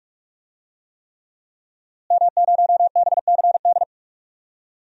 Morse code, heard as a keyed tone, about 46 wpm, 700 Hz. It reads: M0BCD
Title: DXCC-100 Call Signs - 5 Characters - Repeated 3x - 45wpm